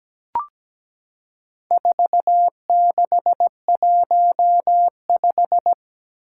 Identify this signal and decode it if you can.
Morse code, heard as a keyed tone, about 17 wpm, 700 Hz. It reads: 4615
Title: Four Challenging Letter-Number Combinations 17wpm